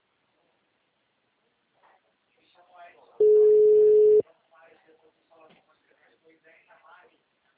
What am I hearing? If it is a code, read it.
Ringback tone (Europe, 425 Hz), with speech.
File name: out- -2033- - - .41620